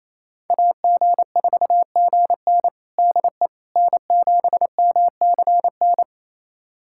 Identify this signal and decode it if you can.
Morse code, heard as a keyed tone, about 28 wpm, 700 Hz. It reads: AG4GN DE N7MCN
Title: USA Callsign DE Callsign - 5 Characters - 28wpm